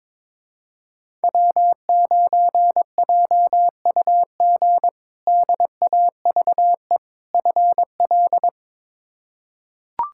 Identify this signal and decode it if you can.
Morse code, heard as a keyed tone, about 21 wpm, 700 Hz. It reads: W9JUG DAVE FL
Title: K1USN SST QSO Practice - 22wpm